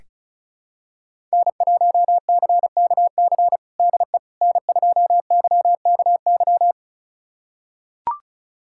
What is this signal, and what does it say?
Morse code, heard as a keyed tone, about 36 wpm, 700 Hz. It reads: N1CKC DE N2YKY
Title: USA Callsign DE Callsign - 5 Characters - 35wpm